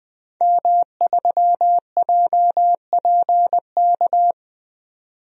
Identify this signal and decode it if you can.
Morse code, heard as a keyed tone, about 20 wpm, 700 Hz. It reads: M3JPK
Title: DXCC-100 Call Signs - 5 Characters - 20wpm